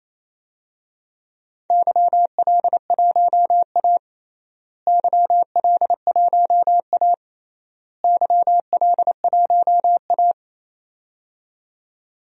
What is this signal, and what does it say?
Morse code, heard as a keyed tone, about 28 wpm, 700 Hz. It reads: YL1A YL1A YL1A
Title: DXCC-100 Call Signs - 4 Characters - Repeated 3x - 28wpm